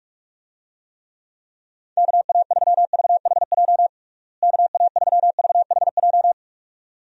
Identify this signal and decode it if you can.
Morse code, heard as a keyed tone, about 46 wpm, 700 Hz. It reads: KA3VHJ KA3VHJ
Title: USA Call Signs - 6 Characters - Repeated 2x - 45wpm